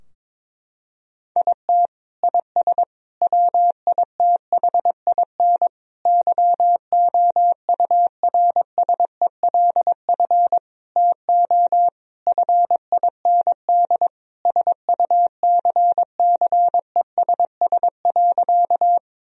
Morse code, heard as a keyed tone, about 22 wpm, 700 Hz. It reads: IT IS WITHIN YOURSELF TO FIND SUCCESS.
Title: Sentences from Top 1000 Words - Review Part 4 - 22wpm